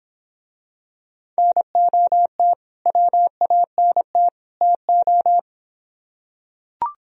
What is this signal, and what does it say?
Morse code, heard as a keyed tone, about 26 wpm, 700 Hz. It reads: NOT WANT TO